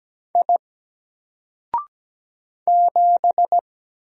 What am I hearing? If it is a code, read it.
Morse code, heard as a keyed tone, about 17 wpm, 700 Hz. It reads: I 7
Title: Single Character (Letter:Number:Punctuation:Prosign) 17wpm